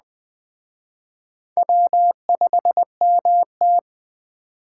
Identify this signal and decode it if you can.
Morse code, heard as a keyed tone, about 20 wpm, 700 Hz. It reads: W5MT